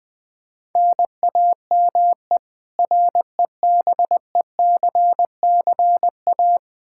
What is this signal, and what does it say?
Morse code, heard as a keyed tone, about 20 wpm, 700 Hz. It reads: NAME REBECCA
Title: K1USN SST QSO Practice 20wpm